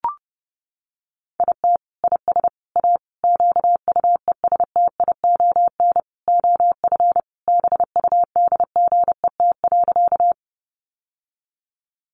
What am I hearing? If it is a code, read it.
Morse code, heard as a keyed tone, about 28 wpm, 700 Hz. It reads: IT IS A QUESTION OF BUDGET.
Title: Sentences from Top 1200 Words 30wpm